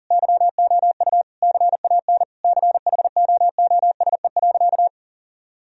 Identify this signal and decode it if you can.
Morse code, heard as a keyed tone, about 40 wpm, 700 Hz. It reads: YOU CAN CHOOSE.
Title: Sentences from Top 1000 Words - Review Part 4 - 40wpm